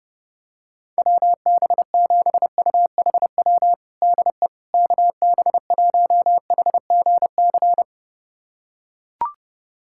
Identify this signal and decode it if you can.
Morse code, heard as a keyed tone, about 30 wpm, 700 Hz. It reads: WB7UHW DE KB1HGC
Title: USA Callsign DE Callsign 30wpm